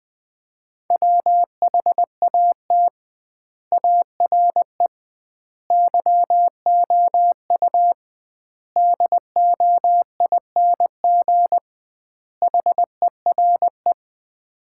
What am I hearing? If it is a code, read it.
Morse code, heard as a keyed tone, about 19 wpm, 700 Hz. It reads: WHAT ARE YOU DOING HERE
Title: 2x Word-Spacing - Sets of 5 Words - 20wpm